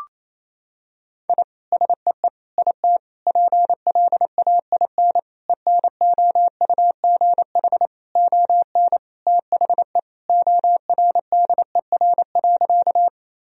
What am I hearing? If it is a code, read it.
Morse code, heard as a keyed tone, about 28 wpm, 700 Hz. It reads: I SEE IT PLAIN ENOUGH ON THE ORDER.
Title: Sentences from Top 1000 Words - Review Part 2 - 28wpm